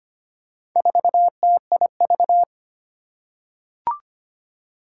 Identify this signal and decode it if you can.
Morse code, heard as a keyed tone, about 25 wpm, 700 Hz. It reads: 4TIV
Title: Four Challenging Letter-Number Combinations 25wpm